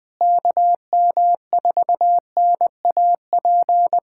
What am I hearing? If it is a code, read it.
Morse code, heard as a keyed tone, about 20 wpm, 700 Hz. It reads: KM4NAP
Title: K1USN SST QSO Practice - 20wpm